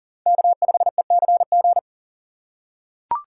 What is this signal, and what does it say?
Morse code, heard as a keyed tone, about 40 wpm, 700 Hz. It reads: K5ECG
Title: USA Call Signs - 5 Characters - 4x Character Spacing - 40wpm